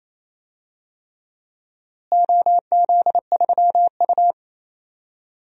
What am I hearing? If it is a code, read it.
Morse code, heard as a keyed tone, about 28 wpm, 700 Hz. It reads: OZ3U
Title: DXCC-100 Call Signs - 4 Characters - Repeated 3x - 28wpm